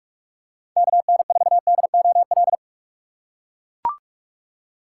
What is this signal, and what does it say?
Morse code, heard as a keyed tone, about 42 wpm, 690 Hz. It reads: KN4DOL
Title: USA Call Signs - 6 Characters - 3x Character Spacing - 45wpm